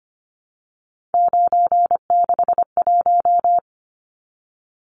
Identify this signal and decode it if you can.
Morse code, heard as a keyed tone, about 25 wpm, 700 Hz. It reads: 961
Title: Numbers - 3-Digit 25wpm